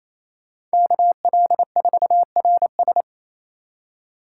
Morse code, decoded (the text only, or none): KL4RS